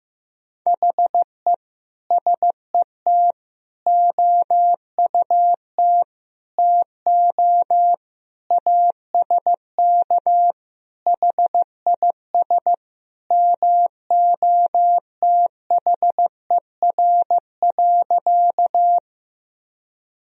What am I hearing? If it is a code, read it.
Morse code, heard as a keyed tone, about 15 wpm, 700 Hz. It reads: HE SET OUT TO ASK HIS MOTHER.
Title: Sentences from Top 1000 Words - Review Part 1 - 15wpm